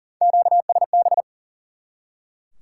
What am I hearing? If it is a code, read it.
Morse code, heard as a keyed tone, about 40 wpm, 700 Hz. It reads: QSB
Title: Top 90 QSO Elements 40wpm